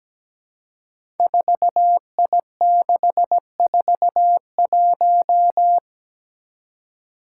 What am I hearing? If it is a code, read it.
Morse code, heard as a keyed tone, about 17 wpm, 700 Hz. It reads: 4I641
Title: Five Challenging Letter-Number Combinations 17wpm